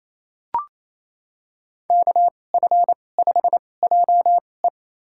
Morse code, heard as a keyed tone, about 28 wpm, spaced as for 20 wpm, 700 Hz. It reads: KF5JE
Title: USA Call Signs - 5 Characters - 2x Character Spacing - 28wpm